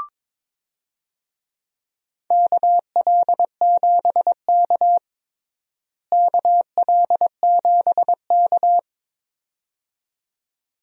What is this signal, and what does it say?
Morse code, heard as a keyed tone, about 22 wpm, 700 Hz. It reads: KL7K KL7K
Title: DXCC-100 Call Signs - 4 Characters - Repeated 2x - 22wpm